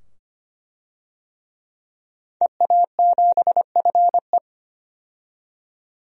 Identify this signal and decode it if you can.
Morse code, heard as a keyed tone, about 25 wpm, 700 Hz. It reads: EA7FE
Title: DXCC-100 Call Signs - 5 Characters - 25wpm